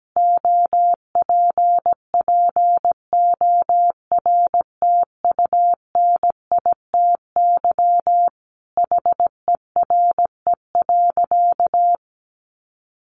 Morse code, heard as a keyed tone, about 17 wpm, 700 Hz. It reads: OPPORTUNITY HERE.
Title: Sentences from Top 1300 Words 17wpm